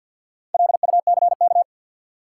Morse code, heard as a keyed tone, about 51 wpm, 700 Hz. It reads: LUCK
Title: Sentences from Top 1500 Words - New Word Drill Alt - 50wpm